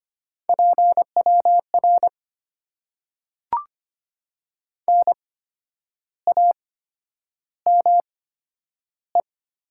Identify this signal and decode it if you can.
Morse code, heard as a keyed tone, about 25 wpm, spaced as for 10 wpm, 700 Hz. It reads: PWR NAME
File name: Top 90 QSO Elements - 8x Character Spacing - 25wpm